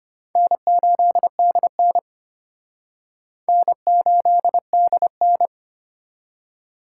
Morse code, heard as a keyed tone, about 33 wpm, 700 Hz. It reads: N8DN N8DN